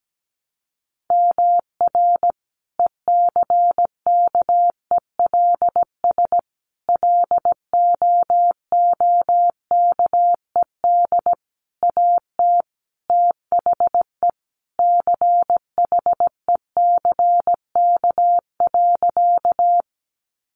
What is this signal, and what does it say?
Morse code, heard as a keyed tone, about 17 wpm, 700 Hz. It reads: MR ECKELS LOOKED AT THE CHECK.
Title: A Sound Of Thunder 17wpm